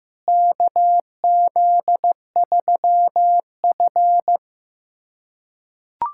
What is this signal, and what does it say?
Morse code, heard as a keyed tone, about 15 wpm, 700 Hz. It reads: KZ3F
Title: USA Callsign DE Callsign - 4 Characters - 15wpm